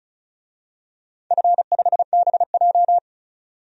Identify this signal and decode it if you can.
Morse code, heard as a keyed tone, about 36 wpm, 700 Hz. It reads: F5BJ
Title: DXCC-100 Call Signs - 4 Characters - 35wpm